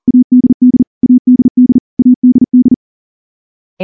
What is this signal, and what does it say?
Morse code, heard as a keyed tone, about 38 wpm, 250 Hz. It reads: ADD ADD ADD
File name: morse / morse-training-40WPM